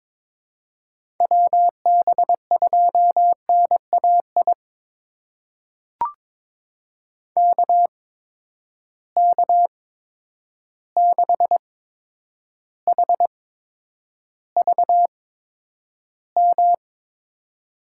Morse code, heard as a keyed tone, about 21 wpm, spaced as for 9 wpm, 700 Hz. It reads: WB2NAI KK6HVM
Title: USA Call Signs - 6 Characters - 8x Character Spacing - 22wpm